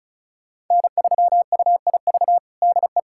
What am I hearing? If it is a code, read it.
Morse code, heard as a keyed tone, about 36 wpm, 700 Hz. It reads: N3UIV DE
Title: USA Callsign DE Callsign - 5 Characters - 35wpm